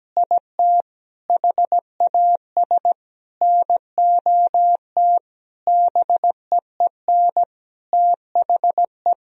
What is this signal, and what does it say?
Morse code, heard as a keyed tone, about 17 wpm, 700 Hz. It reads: IT HAS NOT BEEN THE
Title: Sentences from Top 1000 Words - Review Part 1 - 17wpm